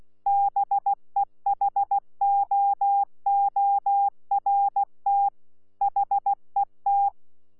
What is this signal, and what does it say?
Morse code, heard as a keyed tone, about 16 wpm, 800 Hz. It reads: BEHOORT HET
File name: Z PI4VRZ / PI4VRZ 14-jun-2025